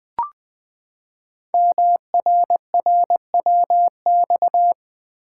Morse code, heard as a keyed tone, about 20 wpm, 700 Hz. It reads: MRRWX
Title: Code Groups - 5 Letters - 20wpm